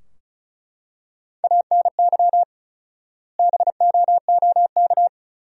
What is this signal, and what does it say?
Morse code, heard as a keyed tone, about 36 wpm, 700 Hz. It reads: ANY BOOK
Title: 4x Word-Spacing - Sentences from Top 300 Words - 35wpm